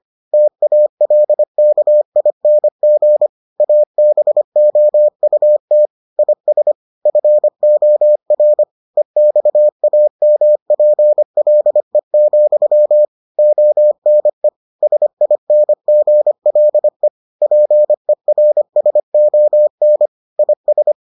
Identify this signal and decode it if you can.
Morse code, heard as a keyed tone, about 25 wpm, 600 Hz. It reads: TALKING ABOUT IS FOR EXAMPLE, ONE SINGLE PERSON IS